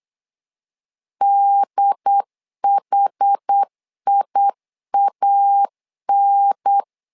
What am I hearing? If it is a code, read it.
Morse code, heard as a keyed tone, about 8 wpm, 800 Hz. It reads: DHIAN
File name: 4 Q 8wpm